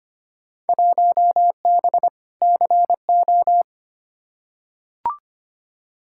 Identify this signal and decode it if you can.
Morse code, heard as a keyed tone, about 25 wpm, 700 Hz. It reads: 1B CO
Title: Field Day Exchanges - 25wpm